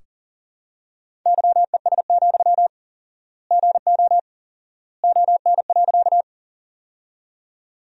Morse code, heard as a keyed tone, about 40 wpm, 700 Hz. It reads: YES, GO ON.
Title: 4x Word-Spacing - Sentences from Top 500 Words - 40wpm